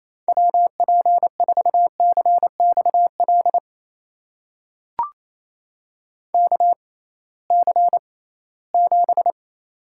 Morse code, heard as a keyed tone, about 28 wpm, spaced as for 18 wpm, 700 Hz. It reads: WP4CXL KC7
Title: USA Call Signs - 6 Characters - 6x Character Spacing - 28wpm